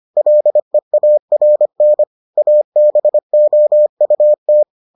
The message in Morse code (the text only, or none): LEARN ABOUT